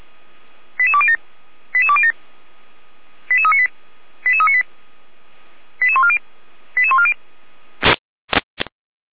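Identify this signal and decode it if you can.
Five-tone selective call (ZVEI), heard as quick runs of five tones, tones of about 70 ms. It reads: 88207 88207 88398 88398 88140 88140
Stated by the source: ZVEI five-tone call received on SDR.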